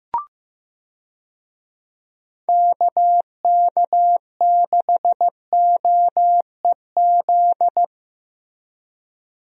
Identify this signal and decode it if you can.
Morse code, heard as a keyed tone, about 15 wpm, 700 Hz. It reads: KK6OEZ
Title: USA Call Signs - 6 Characters - Repeated 3x - 15wpm